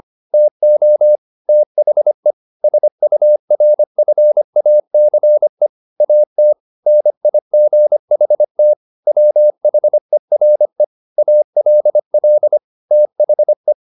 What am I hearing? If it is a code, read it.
Morse code, heard as a keyed tone, about 25 wpm, 600 Hz. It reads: TO THE SURFACE AT NIGHT WHERE ALL THE
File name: the old man and the sea part 1 25wpm